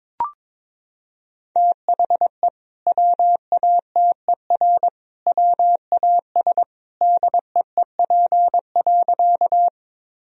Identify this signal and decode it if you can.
Morse code, heard as a keyed tone, about 22 wpm, 700 Hz. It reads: THE WATER WAS DEEP.
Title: Sentences from Top 1000 Words - Review Part 3 - 22wpm